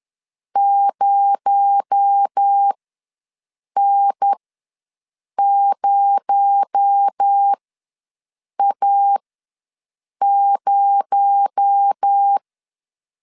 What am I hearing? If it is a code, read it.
Morse code, heard as a keyed tone, about 10 wpm, spaced as for 6 wpm, 800 Hz. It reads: 0N0A0
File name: LEZIONI / 6 PKX0 / 6 PKX0 ws 10wpm 4220